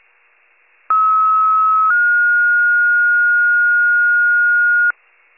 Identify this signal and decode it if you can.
Two-tone page, heard as a one-second tone, then a three-second tone, about 1290 then 1430 Hz. It is heard from 0.9 to 4.9 s.